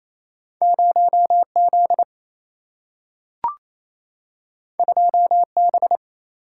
Morse code, heard as a keyed tone, about 28 wpm, 700 Hz. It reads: 0Z 2B
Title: Single-Letter Single-Number 28wpm